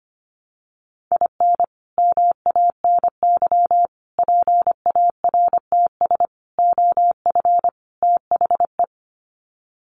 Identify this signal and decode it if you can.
Morse code, heard as a keyed tone, about 25 wpm, 700 Hz. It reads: IN MANY PARTS OF THE
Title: Sets of 5 Words Encore 25wpm